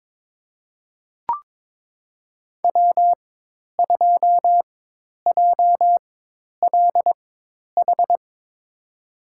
Morse code, heard as a keyed tone, about 22 wpm, spaced as for 10 wpm, 700 Hz. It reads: W2JLH
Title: USA Call Signs - 5 Characters - 4x Character Spacing - 22wpm